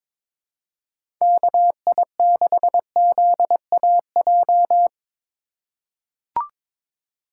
Morse code, heard as a keyed tone, about 22 wpm, 700 Hz. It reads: KI6ZAJ
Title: USA Call Signs - 6 Characters - 8x Character Spacing - 22wpm